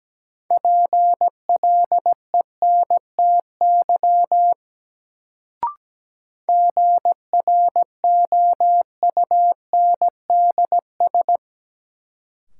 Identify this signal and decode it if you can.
Morse code, heard as a keyed tone, about 17 wpm, 700 Hz. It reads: PLENTY GROUNDS
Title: Sentences from Top 1300 Words - New Word Drill Alt - 17wpm